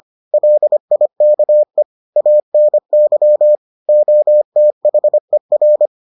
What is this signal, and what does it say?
Morse code, heard as a keyed tone, about 25 wpm, 600 Hz. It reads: LIKE ANY OTHER